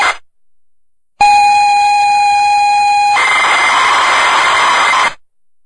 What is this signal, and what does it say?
ZX Spectrum tape loading.